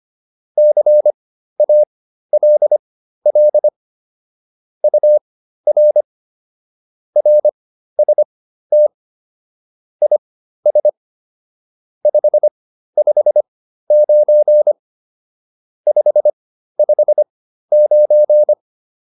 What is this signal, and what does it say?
Morse code, heard as a keyed tone, about 25 wpm, spaced as for 13 wpm, 600 Hz. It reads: CALL UR RST IS 559 559